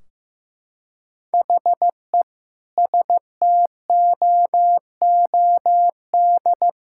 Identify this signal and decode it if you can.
Morse code, heard as a keyed tone, about 15 wpm, 700 Hz. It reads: HE STOOD
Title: Sentences from Top 1600 Words 15wpm